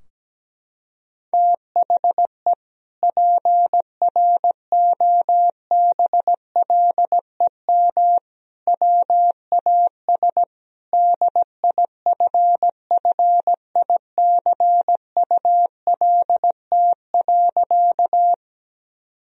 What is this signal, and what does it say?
Morse code, heard as a keyed tone, about 17 wpm, 700 Hz. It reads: THE PROBLEM WAS DIFFICULT.
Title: Sentences from Top 1000 Words - Review Part 2 - 17wpm